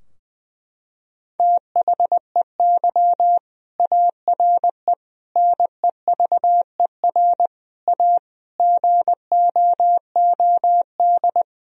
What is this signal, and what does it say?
Morse code, heard as a keyed tone, about 20 wpm, 700 Hz. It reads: THEY ARE NEVER A GOOD
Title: Sentences from Top 1000 Words - Review Part 4 - 20wpm